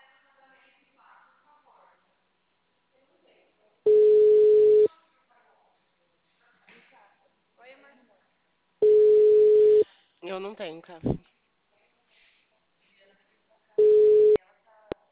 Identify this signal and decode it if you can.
Ringback tone (Europe, 425 Hz), with speech.
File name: out- -2007- - - .11